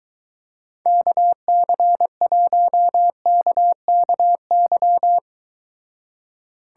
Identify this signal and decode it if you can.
Morse code, heard as a keyed tone, about 23 wpm, 700 Hz. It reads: KC1KKY